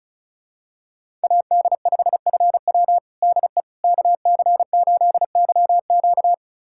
Morse code, heard as a keyed tone, about 36 wpm, 700 Hz. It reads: AD5FW DE KC8YQ
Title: USA Callsign DE Callsign - 5 Characters - 35wpm